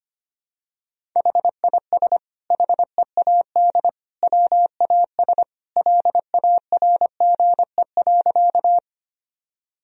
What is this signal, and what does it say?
Morse code, heard as a keyed tone, about 25 wpm, 700 Hz. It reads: HIS HEAD WAS LARGE.
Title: Sentences from Top 1000 Words - Review Part 1 - 25wpm